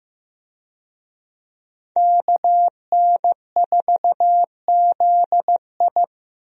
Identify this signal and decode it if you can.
Morse code, heard as a keyed tone, about 15 wpm, 700 Hz. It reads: KN4ZI
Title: USA Call Signs - 5 Characters - Repeated 2x - 15wpm